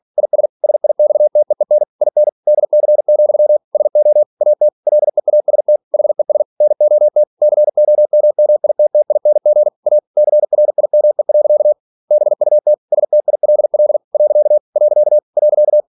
Morse code, heard as a keyed tone, about 46 wpm, 600 Hz. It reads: IS SIXTEEN IN DC, SO AT LEAST HES NOT COMMITTING A CRIME. BUT STILL . .
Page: Never 47wpm